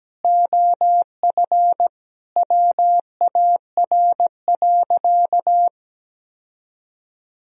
Morse code, heard as a keyed tone, about 17 wpm, 700 Hz. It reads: OF WAR.
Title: Sentences from Top 1000 Words - Review Part 2 - 17wpm